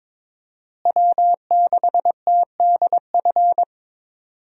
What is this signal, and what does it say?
Morse code, heard as a keyed tone, about 21 wpm, 700 Hz. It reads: W6TDF
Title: USA Call Signs - 5 Characters - Repeated 3x - 22wpm